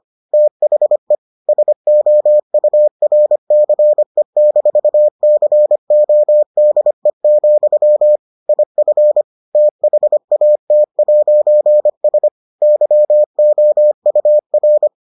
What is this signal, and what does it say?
Morse code, heard as a keyed tone, about 25 wpm, 600 Hz. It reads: THE SOURCE-CODE, IF THAT'S YOUR